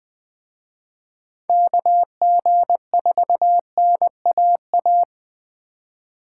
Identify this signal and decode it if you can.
Morse code, heard as a keyed tone, about 21 wpm, 700 Hz. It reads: KG4NAA
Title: Speed Racing - USA 2x3 Call Signs 17wpm